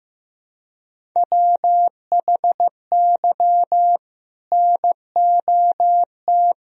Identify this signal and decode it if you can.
Morse code, heard as a keyed tone, about 15 wpm, 700 Hz. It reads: WHY NOT
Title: Sentences from Top 1000 Words - Review Part 3 - 15wpm